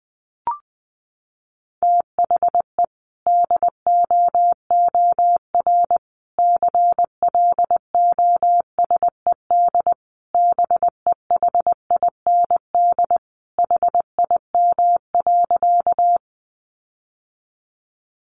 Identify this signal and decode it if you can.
Morse code, heard as a keyed tone, about 20 wpm, 700 Hz. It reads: THE DOOR CLOSED BEHIND HIM.
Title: Sentences from Top 1400 Words 20wpm